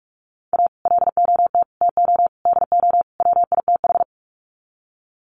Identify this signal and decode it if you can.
Morse code, heard as a keyed tone, about 44 wpm, 700 Hz. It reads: A LOT TO DO WITH
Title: Sets of 5 Words 45wpm